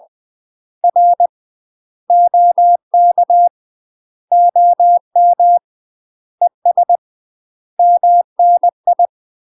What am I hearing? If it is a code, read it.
Morse code, heard as a keyed tone, about 20 wpm, 700 Hz. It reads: R OK OM ES MNI
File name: UK MASTER QSOs TRAINING/20WPM